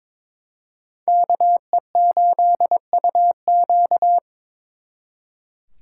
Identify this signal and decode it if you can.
Morse code, heard as a keyed tone, about 26 wpm, 700 Hz. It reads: KE8UQ
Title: Speed Racing - USA Call Signs 22wpm